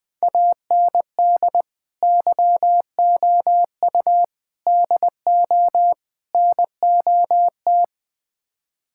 Morse code, heard as a keyed tone, about 20 wpm, 690 Hz. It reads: AND YOU DO NOT
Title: Build A Phrase - 4-Word Phrases - 20wpm